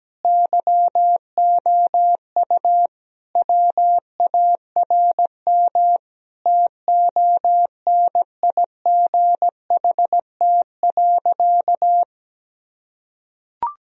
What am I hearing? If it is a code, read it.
Morse code, heard as a keyed tone, about 17 wpm, 700 Hz. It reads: YOU WARM TONIGHT.
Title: Sentences from Top 1000 Words - Review Part 3 - 17wpm